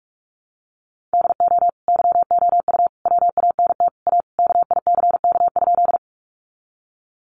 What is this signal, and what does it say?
Morse code, heard as a keyed tone, about 44 wpm, 700 Hz. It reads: DO YOU WANT A KICK?
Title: Sentences from Top 1300 Words 45wpm